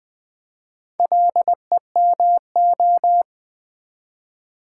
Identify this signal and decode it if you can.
Morse code, heard as a keyed tone, about 20 wpm, 700 Hz. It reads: LEMO
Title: Build A Word - 1 Letter At A Time - 5-Letter Words - 20wpm